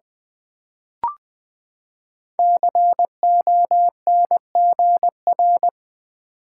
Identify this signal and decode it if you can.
Morse code, heard as a keyed tone, about 20 wpm, 700 Hz. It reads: CONGR
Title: Build A Word - 5 Letters At A Time - 12-Letter Words - 20wpm